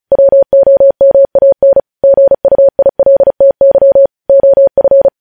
Morse code, heard as a keyed tone, about 36 wpm, 550 Hz. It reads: WOMAN GUILTY OF
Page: news headlines in morse code at 35 wpm